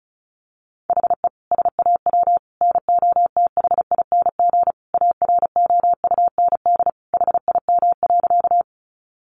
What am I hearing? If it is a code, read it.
Morse code, heard as a keyed tone, about 34 wpm, 700 Hz. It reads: HE SAW NOTHING AROUND HIM.
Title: Sentences from Top 600 Words 35wpm